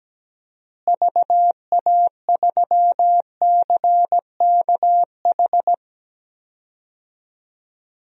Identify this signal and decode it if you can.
Morse code, heard as a keyed tone, about 17 wpm, 700 Hz. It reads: VA3CKH
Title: DXCC-100 Call Signs - 6 Characters - Repeated 3x - 17wpm